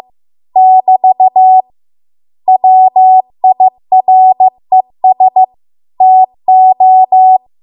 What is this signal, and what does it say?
Morse code, heard as a keyed tone, about 15 wpm, 750 Hz. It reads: <BT> WIRES TO
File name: Archive / 15 WPM / 15WPM